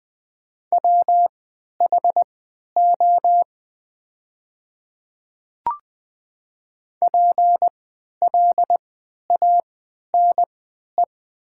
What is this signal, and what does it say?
Morse code, heard as a keyed tone, about 20 wpm, spaced as for 11 wpm, 700 Hz. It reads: WHO PLANE
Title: 3x Character Spacing - Top 500 Words - 20wpm